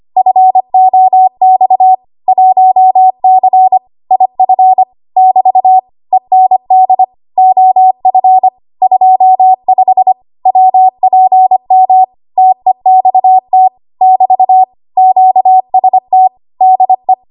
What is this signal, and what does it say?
Morse code, heard as a keyed tone, about 25 wpm, 750 Hz. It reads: FOX 1C IF <BT> END OF 25 WPM TEXT <BT> QST DE
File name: Archive / 25 WPM / 25WPM